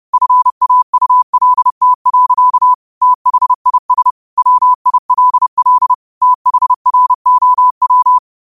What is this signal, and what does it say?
Morse code, heard as a keyed tone, about 30 wpm, 1000 Hz. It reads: FAULT. THIS WILL THROW
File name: qotdfiles / 30wpm